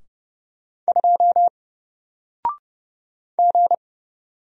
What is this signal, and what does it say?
Morse code, heard as a keyed tone, about 30 wpm, 700 Hz. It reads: 2 G
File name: Single-Letter-Number-30wpm